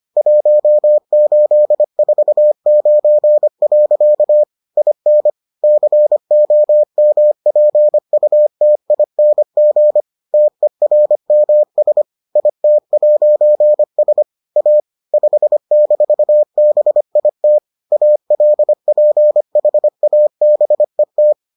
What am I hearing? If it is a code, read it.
Morse code, heard as a keyed tone, about 25 wpm, 600 Hz. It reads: 1849. IN COMPUTING TERMS IT'S A 5-BIT ALPHABET